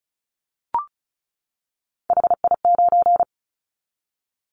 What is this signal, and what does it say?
Morse code, heard as a keyed tone, about 36 wpm, 700 Hz. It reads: HI9
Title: Callsign Prefixes for Top 100 DXCC - 35wpm